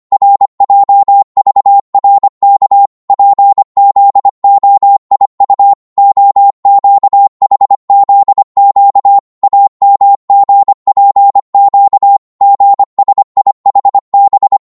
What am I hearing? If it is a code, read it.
Morse code, heard as a keyed tone, about 25 wpm, 800 Hz. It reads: RJVRK PZOIU OQHZQ AMGPQ GSIHB
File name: Morse / ltr-25wpm